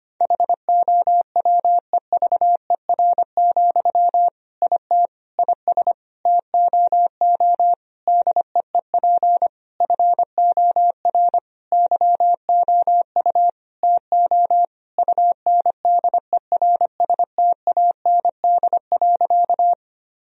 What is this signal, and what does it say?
Morse code, heard as a keyed tone, about 25 wpm, 700 Hz. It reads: HOWEVER, IT IS TOO DEEP FOR YOU TO UNDERSTAND.
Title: Sentences from Top 1000 Words - Review Part 3 - 25wpm